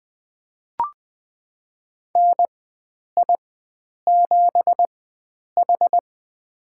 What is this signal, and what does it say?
Morse code, heard as a keyed tone, about 20 wpm, spaced as for 9 wpm, 700 Hz. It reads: NI7H
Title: USA Call Signs - 4 Characters - 4x Character Spacing - 20wpm